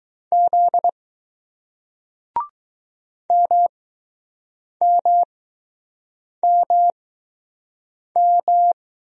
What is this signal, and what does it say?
Morse code, heard as a keyed tone, about 20 wpm, spaced as for 6 wpm, 700 Hz. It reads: Z MMMM